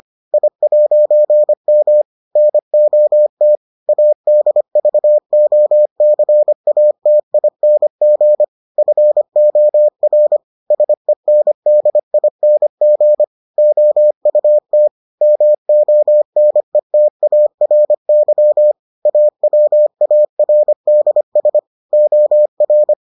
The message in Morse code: I'M NOT ADVOCATING FOR SENDING OUT MONETARY AWARDS OR

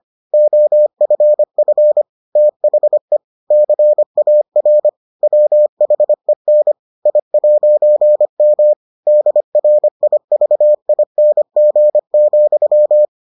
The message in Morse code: OFF THE CAR WHEN I'M DRIVING,

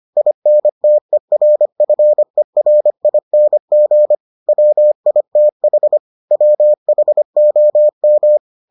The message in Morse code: INTERFERING WITH WHOM

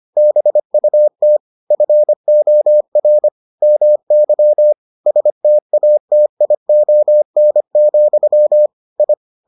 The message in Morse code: BUT FOR MY STATION, I

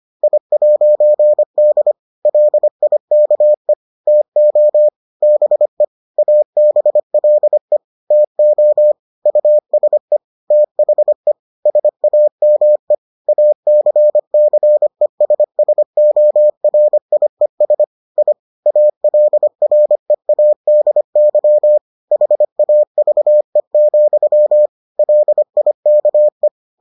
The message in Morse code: I'D LIKE TO BE ABLE TO USE THE SAME ACCESSORIES I ALREADY HAVE, LIKE

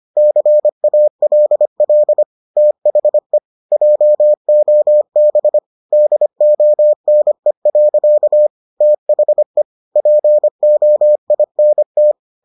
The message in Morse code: CALL THE JOB DONE. THE POINT

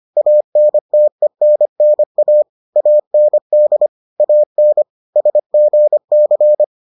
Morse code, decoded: ANTENNA AND AN SGC

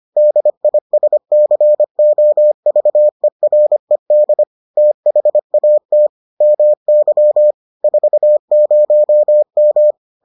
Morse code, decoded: DISCOVERED THAT MY 40M

